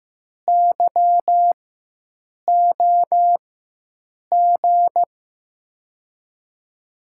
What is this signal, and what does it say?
Morse code, heard as a keyed tone, about 15 wpm, spaced as for 7 wpm, 700 Hz. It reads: YOG
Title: USA Call Signs - 6 Characters - 4x Character Spacing - 15wpm